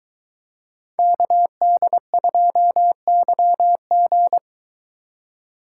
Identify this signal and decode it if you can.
Morse code, heard as a keyed tone, about 21 wpm, 700 Hz. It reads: KD2YG